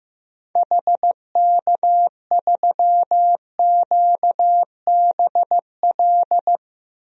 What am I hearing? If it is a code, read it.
Morse code, heard as a keyed tone, about 15 wpm, 700 Hz. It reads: HK3QBL
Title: DXCC-100 Call Signs - 6 Characters - Repeated 2x - 15wpm